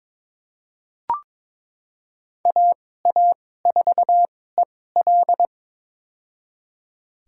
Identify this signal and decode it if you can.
Morse code, heard as a keyed tone, about 23 wpm, spaced as for 16 wpm, 700 Hz. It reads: AA4EL
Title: USA Call Signs - 2x Character Spacing - 22wpm